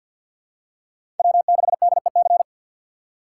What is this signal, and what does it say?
Morse code, heard as a keyed tone, about 49 wpm, 700 Hz. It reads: W6DEC
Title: USA Call Signs - 5 Characters - 2x Character Spacing - 50wpm